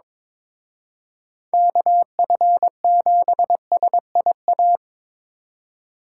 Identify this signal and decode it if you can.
Morse code, heard as a keyed tone, about 26 wpm, 700 Hz. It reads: KF7SIA
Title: Speed Racing - USA 2x3 Call Signs 22wpm